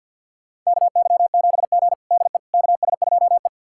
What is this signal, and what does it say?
Morse code, heard as a keyed tone, about 52 wpm, 700 Hz. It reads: KY7G DE KS2E
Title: USA Callsign DE Callsign - 4 Characters - 50wpm